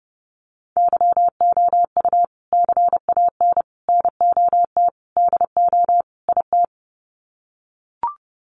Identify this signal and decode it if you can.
Morse code, heard as a keyed tone, about 30 wpm, 700 Hz. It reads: YOU CAN NOT DO IT
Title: Sets of 5 Words Encore2 30wpm